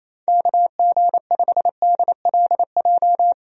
Morse code, heard as a keyed tone, about 28 wpm, 700 Hz. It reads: KG5DLJ
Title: USA Call Signs - 6 Characters - 28wpm